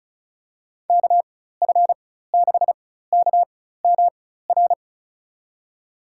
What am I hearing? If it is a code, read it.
Morse code, heard as a keyed tone, about 34 wpm, spaced as for 16 wpm, 700 Hz. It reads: KF6KMR
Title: USA Call Signs - 6 Characters - 4x Character Spacing - 35wpm